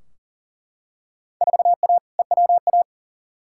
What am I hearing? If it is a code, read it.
Morse code, heard as a keyed tone, about 40 wpm, 700 Hz. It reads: 4A EWA
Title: Field Day Exchanges - 40wpm